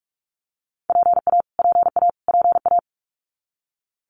Morse code, heard as a keyed tone, about 44 wpm, 700 Hz. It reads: PA PA PA
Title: US States Repeated 3x 45wpm